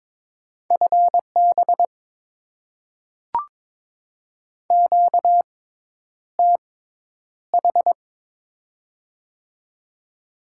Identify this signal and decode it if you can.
Morse code, heard as a keyed tone, about 22 wpm, spaced as for 10 wpm, 700 Hz. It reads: FB QTH
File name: Top 20 QSO Elements - 6x Character Spacing - 22wpm